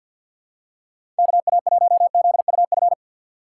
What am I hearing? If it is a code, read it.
Morse code, heard as a keyed tone, about 52 wpm, 700 Hz. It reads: KA1ZUF
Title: USA Call Signs - 6 Characters - 3x Character Spacing - 50wpm